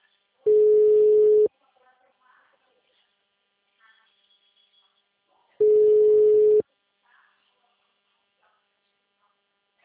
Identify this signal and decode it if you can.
Ringback tone (Europe, 425 Hz), with speech.
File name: out- -2006- - - .17